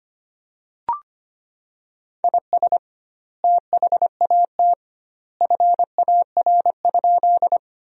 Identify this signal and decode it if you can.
Morse code, heard as a keyed tone, about 25 wpm, 700 Hz. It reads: IS THAT FAR?
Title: 2x Word-Spacing - Sentences from Top 300 Words - 25wpm